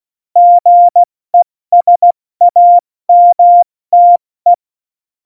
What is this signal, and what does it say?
Morse code, heard as a keyed tone, about 16 wpm, 700 Hz. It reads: GESAMTE
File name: Morsekurs / Witze / Witz 18. Dezember, 70 F80